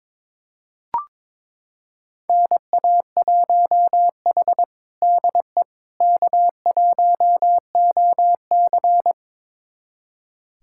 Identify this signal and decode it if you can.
Morse code, heard as a keyed tone, about 22 wpm, 700 Hz. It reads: NA1H DE K1OC
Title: USA Callsign DE Callsign - 4 Characters - 22wpm